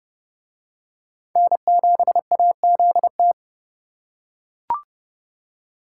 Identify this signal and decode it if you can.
Morse code, heard as a keyed tone, about 30 wpm, 700 Hz. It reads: N7AZT